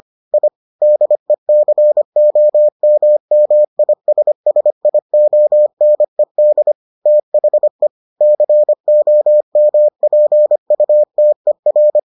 Morse code, heard as a keyed tone, about 25 wpm, 600 Hz. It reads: I DECOMMISSIONED THE COMPUTER